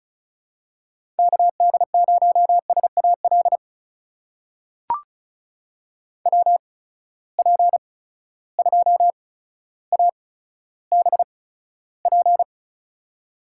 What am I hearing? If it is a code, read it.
Morse code, heard as a keyed tone, about 36 wpm, spaced as for 15 wpm, 700 Hz. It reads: KD0SAL WP2ABP
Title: USA Call Signs - 6 Characters - 8x Character Spacing - 35wpm